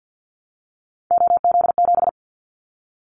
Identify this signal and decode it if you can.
Morse code, heard as a keyed tone, about 50 wpm, 700 Hz. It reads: OZ7
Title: Callsign Prefixes for Top 100 DXCC - 50wpm